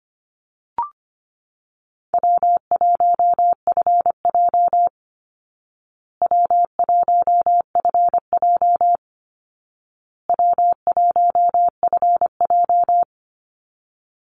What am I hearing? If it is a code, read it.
Morse code, heard as a keyed tone, about 25 wpm, 700 Hz. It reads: W1FJ W1FJ W1FJ
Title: Easy Call Signs Repeated 3x - 25wpm